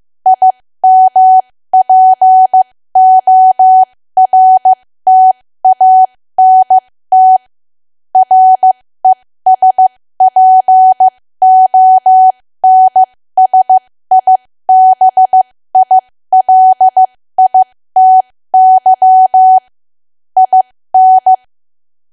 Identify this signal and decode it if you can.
Morse code, heard as a keyed tone, about 15 wpm, spaced as for 13 wpm, 750 Hz. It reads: IMPORTANT RESPONSIBILITY IN
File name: cwa / W1AW-Level2 / 13WPM